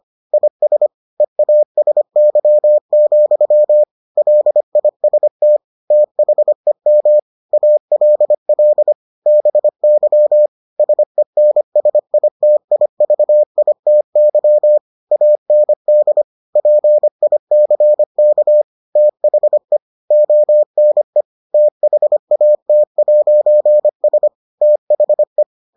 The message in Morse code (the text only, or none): IS EASY, LIST THEM ALL BY SENSITIVITY AND PICK THE ONE THAT'S THE